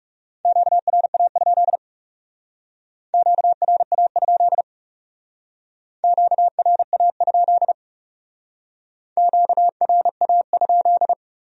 Morse code, heard as a keyed tone, about 40 wpm, 700 Hz. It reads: QRA? QRA? QRA? QRA?